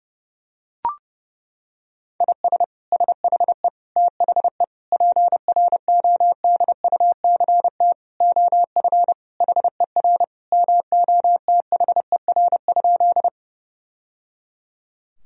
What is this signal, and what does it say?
Morse code, heard as a keyed tone, about 30 wpm, 700 Hz. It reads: IS SHE THE PRODUCT OF HER MOTHER?
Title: Sentences from Top 1000 Words - Review Part 2 - 30wpm